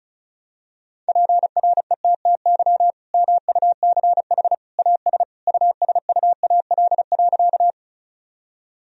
Morse code, heard as a keyed tone, about 34 wpm, 700 Hz. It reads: PRETTY MUCH AS USUAL.
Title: Sentences from Top 1000 Words - Review Part 2 - 35wpm